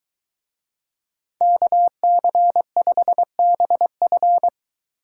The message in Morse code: KC5BF

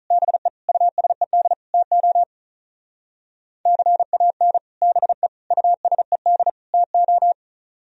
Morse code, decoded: BE USED TO CAN BE USED TO

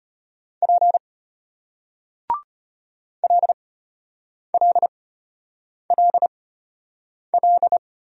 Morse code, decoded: P LLLL

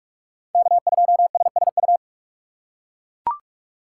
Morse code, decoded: K2SSU

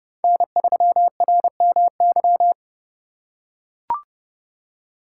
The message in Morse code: N3RMY

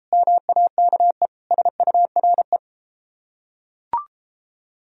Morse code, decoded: MAKE SURE

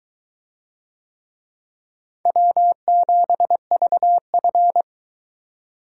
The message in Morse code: W7VF

